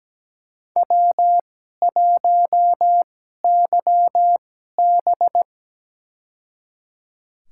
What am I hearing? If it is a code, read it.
Morse code, heard as a keyed tone, about 17 wpm, spaced as for 12 wpm, 700 Hz. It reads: W1YB
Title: USA Call Signs - 2x Character Spacing - 17wpm